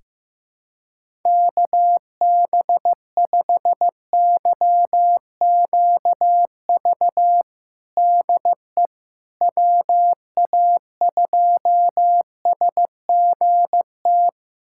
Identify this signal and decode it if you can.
Morse code, heard as a keyed tone, about 15 wpm, 700 Hz. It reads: KB5YQV DE WA2SGT
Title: USA Callsign DE Callsign 15wpm